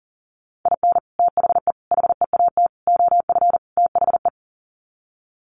Morse code, heard as a keyed tone, about 37 wpm, 700 Hz. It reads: IN THE HEAT OF THE
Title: Sets of 5 Words Encore2 40wpm